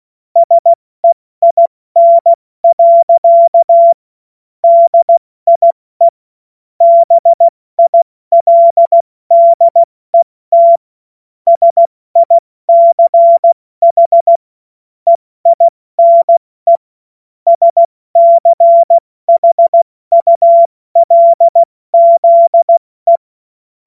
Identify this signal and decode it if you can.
Morse code, heard as a keyed tone, about 16 wpm, 680 Hz. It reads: SEIN. DIE BILDET SICH EINE SCHULZE
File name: Morsekurs / Witze / Witz 19 März 2024 70 F80